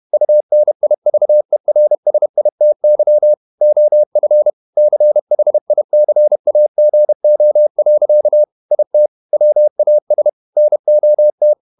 Morse code, heard as a keyed tone, about 31 wpm, 600 Hz. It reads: UNIVERSITY OF CHICAGO. IT WAS NOT
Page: Never 31wpm | CW med Gnister